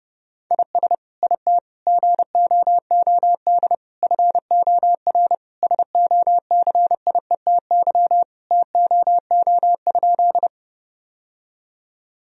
Morse code, heard as a keyed tone, about 32 wpm, 700 Hz. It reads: IS IT GOOD FOR SOCIETY TOO?
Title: Sentences from Top 1000 Words - Review Part 4 - 30wpm